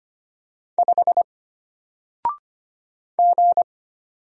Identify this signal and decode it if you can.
Morse code, heard as a keyed tone, about 25 wpm, 700 Hz. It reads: 5 G
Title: Single Character (Letter:Number:Punctuation:Prosign) Alt 25wpm